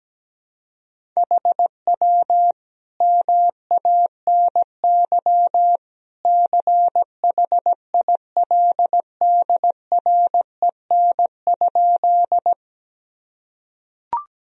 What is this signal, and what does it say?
Morse code, heard as a keyed tone, about 17 wpm, 700 Hz. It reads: HW MANY CHILDREN?
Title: Sentences from Top 300 Words and Abbreviations 17wpm